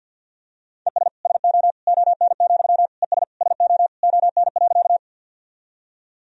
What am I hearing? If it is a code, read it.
Morse code, heard as a keyed tone, about 52 wpm, 700 Hz. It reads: ES SO ON, ES SO ON.
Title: Sentences from Top 100 Words and Abbreviations 50wpm